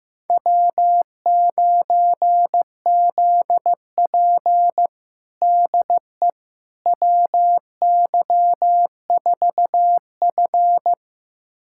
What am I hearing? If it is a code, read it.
Morse code, heard as a keyed tone, about 15 wpm, 700 Hz. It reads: W9ZP DE WY4F
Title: USA Callsign DE Callsign - 4 Characters - 15wpm